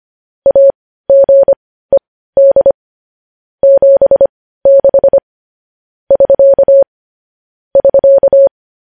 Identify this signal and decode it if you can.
Morse code, heard as a keyed tone, about 25 wpm, spaced as for 15 wpm, 550 Hz. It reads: AGED 76 <SK> <SK>